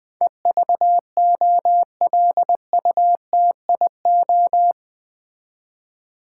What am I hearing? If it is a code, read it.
Morse code, heard as a keyed tone, about 20 wpm, 700 Hz. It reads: EVOLUTIO
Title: Build A Word - 4 Letters At A Time - 12-Letter Words - 20wpm